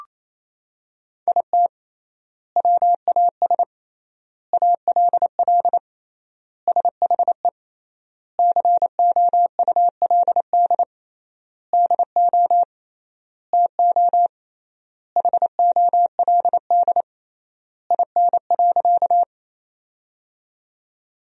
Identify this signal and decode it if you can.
Morse code, heard as a keyed tone, about 27 wpm, 700 Hz. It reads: IT WAS ALL SHE COULD DO TO HOLD IN.
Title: 3x Word-Spacing - Sentences from Top 400 Words - 28wpm